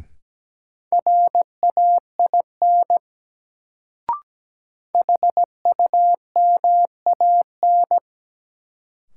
Morse code, heard as a keyed tone, about 17 wpm, 700 Hz. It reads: RAIN HUMAN
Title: Sentences from Top 600 Words - New Word Drill - 17wpm